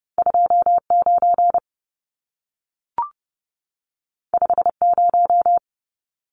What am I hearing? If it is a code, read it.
Morse code, heard as a keyed tone, about 30 wpm, 700 Hz. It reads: 29 50
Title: Numbers - 2-Digit 30wpm